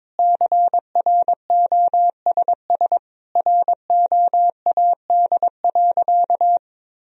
Morse code, heard as a keyed tone, about 21 wpm, 700 Hz. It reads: CROSS ROAD.
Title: Sentences from Top 1000 Words - Review Part 2 - 22wpm